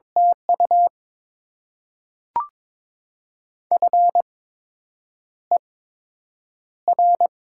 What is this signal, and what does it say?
Morse code, heard as a keyed tone, about 22 wpm, spaced as for 8 wpm, 700 Hz. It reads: TU FER
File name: Top 20 QSO Elements - 8x Character Spacing - 22wpm